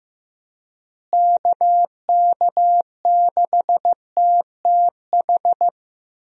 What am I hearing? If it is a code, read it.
Morse code, heard as a keyed tone, about 15 wpm, 700 Hz. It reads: KK6TTH
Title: USA Call Signs - 6 Characters - Repeated 2x - 15wpm